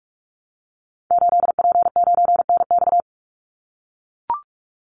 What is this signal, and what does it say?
Morse code, heard as a keyed tone, about 44 wpm, 700 Hz. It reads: 8P9NX
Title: Call Signs - 45wpm